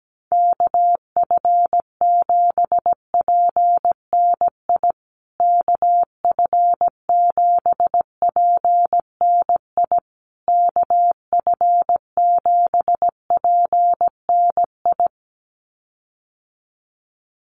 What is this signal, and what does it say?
Morse code, heard as a keyed tone, about 17 wpm, 700 Hz. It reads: KF7PNI KF7PNI KF7PNI
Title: USA Call Signs Repeated 3x 17wpm